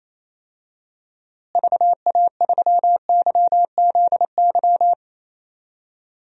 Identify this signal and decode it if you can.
Morse code, heard as a keyed tone, about 28 wpm, 700 Hz. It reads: VA3YZY
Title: DXCC-100 Call Signs - 6 Characters - Repeated 2x - 28wpm